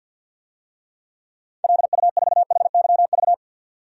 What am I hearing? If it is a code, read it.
Morse code, heard as a keyed tone, about 53 wpm, 700 Hz. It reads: LU4HYV
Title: DXCC-100 Call Signs - 6 Characters - 50wpm